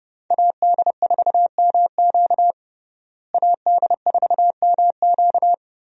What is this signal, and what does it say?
Morse code, heard as a keyed tone, about 30 wpm, 700 Hz. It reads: AD4MQ AD4MQ
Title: USA Call Signs - 5 Characters - Repeated 3x - 30wpm